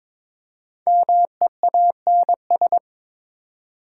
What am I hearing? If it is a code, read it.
Morse code, heard as a keyed tone, about 22 wpm, 700 Hz. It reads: MEANS
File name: 5 Letter Words - First 3 Letters Given - 22wpm